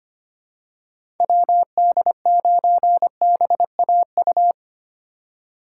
Morse code, heard as a keyed tone, about 25 wpm, 700 Hz. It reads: WD9BAU
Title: USA Call Signs - 6 Characters - 4x Character Spacing - 25wpm